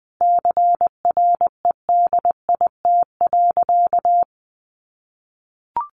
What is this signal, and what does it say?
Morse code, heard as a keyed tone, about 20 wpm, 700 Hz. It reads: CREDIT.
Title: Sentences from Top 1000 Words 20wpm